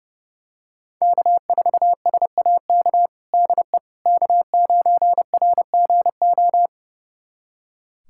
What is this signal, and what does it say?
Morse code, heard as a keyed tone, about 32 wpm, 700 Hz. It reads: K4SAK DE K9RGO
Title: USA Callsign DE Callsign - 5 Characters - 30wpm